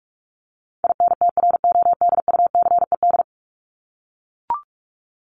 Morse code, heard as a keyed tone, about 44 wpm, 700 Hz. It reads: INTRODUCED
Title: ED Suffix - Top 100 Words with Base Words 45wpm